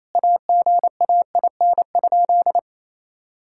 Morse code, heard as a keyed tone, about 28 wpm, 700 Hz. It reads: AGAIN?
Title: Sentences from Top 1600 Words 28wpm